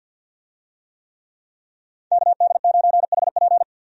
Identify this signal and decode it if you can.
Morse code, heard as a keyed tone, about 52 wpm, 700 Hz. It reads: KD9HP
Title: USA Call Signs - 5 Characters - 50wpm